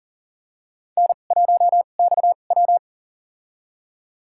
Morse code, heard as a keyed tone, about 40 wpm, spaced as for 29 wpm, 700 Hz. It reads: N1XW
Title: USA Call Signs - 4 Characters - 2x Character Spacing - 40wpm